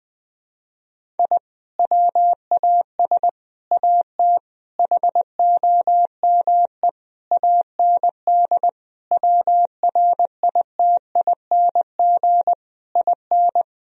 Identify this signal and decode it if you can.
Morse code, heard as a keyed tone, about 20 wpm, 700 Hz. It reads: I WAS AT HOME AND WRITING IN
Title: Sentences from Top 1000 Words - Review Part 1 - 20wpm